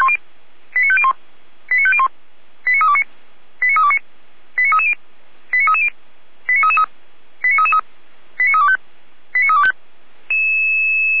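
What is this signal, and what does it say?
Five-tone selective call (ZVEI), heard as quick runs of five tones, tones of about 70 ms. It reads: 89501 89501 89329 89329 89330 89330 89333 89333 89325 89325
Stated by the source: ZVEI five-tone selective call 89320 received on SDR.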